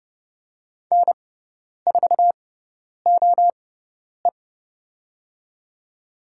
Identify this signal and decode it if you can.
Morse code, heard as a keyed tone, about 30 wpm, spaced as for 10 wpm, 700 Hz. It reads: N4OE
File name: Lesson 18g - Letter m - Call Signs 30wpm